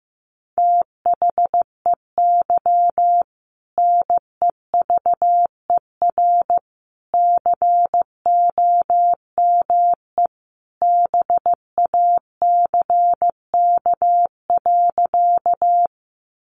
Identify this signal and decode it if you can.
Morse code, heard as a keyed tone, about 14 wpm, 700 Hz. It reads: THEY NEVER COME BACK.
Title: Sentences from Top 300 Words 15wpm